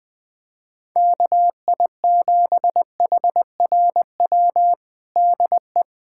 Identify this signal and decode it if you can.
Morse code, heard as a keyed tone, about 20 wpm, 700 Hz. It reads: KI7HRW DE
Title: USA Callsign DE Callsign - 6 Characters - 20wpm